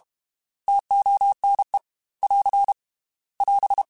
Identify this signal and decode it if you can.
Morse code, heard as a keyed tone, about 33 wpm, 800 Hz. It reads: TONE <AR> <AS>
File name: T Prefixes / T mp3s